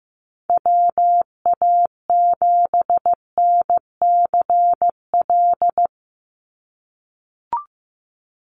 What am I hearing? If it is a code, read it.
Morse code, heard as a keyed tone, about 15 wpm, 700 Hz. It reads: WA7NCL
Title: NA SOTA Activator Callsigns 15wpm